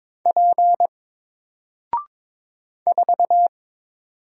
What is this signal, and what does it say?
Morse code, heard as a keyed tone, about 22 wpm, 700 Hz. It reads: P 4
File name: Single-Letter-Number-22wpm